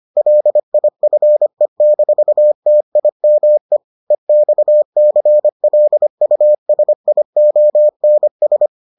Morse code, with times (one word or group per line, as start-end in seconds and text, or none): 0.2-3.8 LIFE-TIME
4.1-8.7 EXCLUSIONS